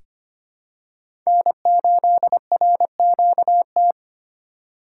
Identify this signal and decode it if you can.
Morse code, heard as a keyed tone, about 25 wpm, 700 Hz. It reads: N8RQT
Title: USA Call Signs - 5 Characters - 4x Character Spacing - 25wpm